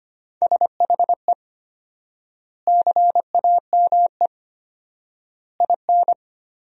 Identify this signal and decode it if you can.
Morse code, heard as a keyed tone, about 25 wpm, 700 Hz. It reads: SHE CAME IN
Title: 4x Word-Spacing - Sentences from Top 300 Words - 25wpm